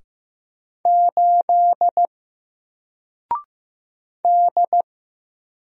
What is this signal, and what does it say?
Morse code, heard as a keyed tone, about 15 wpm, 700 Hz. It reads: 8 D
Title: Single Letter-Number Alt 15wpm